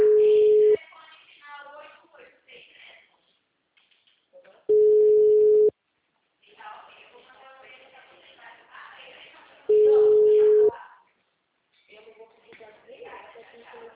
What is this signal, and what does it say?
Ringback tone (Europe, 425 Hz), with speech.